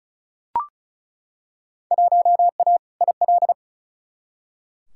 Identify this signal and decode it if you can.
Morse code, heard as a keyed tone, about 36 wpm, 700 Hz. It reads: 1A IL
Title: Field Day Exchanges Encore - 35wpm